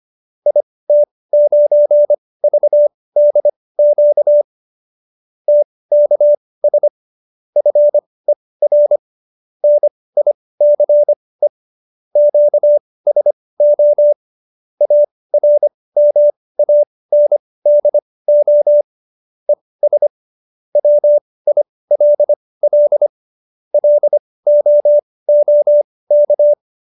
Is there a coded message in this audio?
Yes: Morse code from 0.5 to 26.5 s, heard as a keyed tone, about 25 wpm, spaced as for 18 wpm, 590 Hz. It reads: IT9VDQ TKS FER NICE QSO ARMANDO ES WILL LOOK